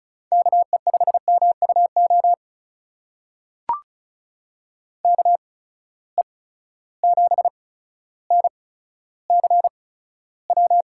Morse code, heard as a keyed tone, about 36 wpm, spaced as for 15 wpm, 700 Hz. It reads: KE5MUO KE7NCW
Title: USA Call Signs - 6 Characters - 8x Character Spacing - 35wpm